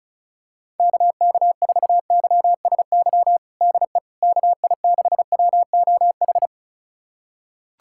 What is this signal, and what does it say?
Morse code, heard as a keyed tone, about 36 wpm, 690 Hz. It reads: KK4YSY DE KI6WOH
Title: USA Callsign DE Callsign 35wpm